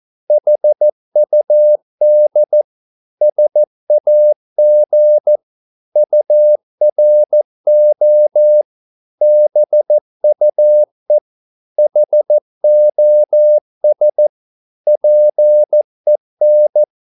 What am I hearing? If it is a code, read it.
Morse code, heard as a keyed tone, about 14 wpm, 600 Hz. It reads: HUD SAG URO BUE HOS PEN
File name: Korte ord DK 14wpm